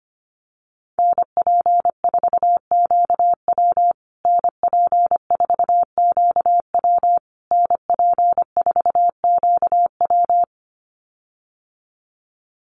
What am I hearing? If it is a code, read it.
Morse code, heard as a keyed tone, about 25 wpm, 700 Hz. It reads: NP4QW NP4QW NP4QW
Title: USA Call Signs Repeated 3x 25wpm